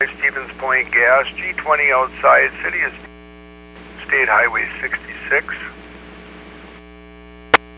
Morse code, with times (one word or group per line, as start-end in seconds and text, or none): none